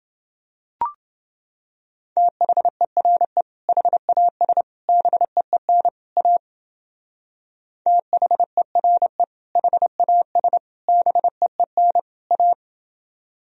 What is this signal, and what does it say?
Morse code, heard as a keyed tone, about 29 wpm, 700 Hz. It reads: THERE HAS BEEN A THERE HAS BEEN A